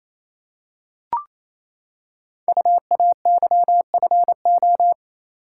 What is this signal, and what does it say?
Morse code, heard as a keyed tone, about 27 wpm, 700 Hz. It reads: UAYFO
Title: Code Groups - 5 Letters - 28wpm